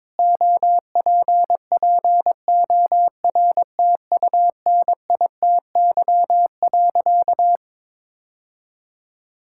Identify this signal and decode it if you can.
Morse code, heard as a keyed tone, about 22 wpm, 700 Hz. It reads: OPPORTUNITY.
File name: practice / 22wpm / Sentences with 4-Word Phrases - 22wpm